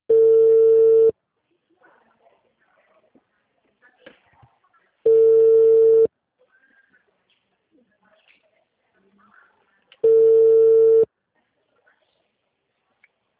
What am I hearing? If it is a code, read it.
Ringback tone (China, 450 Hz), with speech.